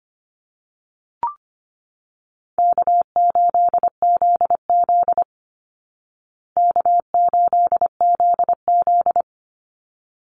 Morse code, heard as a keyed tone, about 25 wpm, 700 Hz. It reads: K8ZZ K8ZZ
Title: Easy Call Signs Repeated 2x - 25wpm